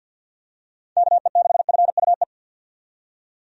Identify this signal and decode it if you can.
Morse code, heard as a keyed tone, about 52 wpm, 700 Hz. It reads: KE6FUE
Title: USA Call Signs - 6 Characters - 50wpm